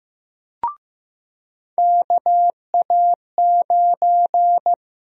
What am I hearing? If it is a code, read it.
Morse code, heard as a keyed tone, about 15 wpm, 700 Hz. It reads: KA9
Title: USA Call Sign Prefixes - 15wpm